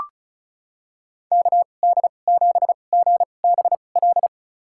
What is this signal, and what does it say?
Morse code, heard as a keyed tone, about 36 wpm, spaced as for 25 wpm, 700 Hz. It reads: KD7GBL
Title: USA Call Signs - 6 Characters - 2x Character Spacing - 35wpm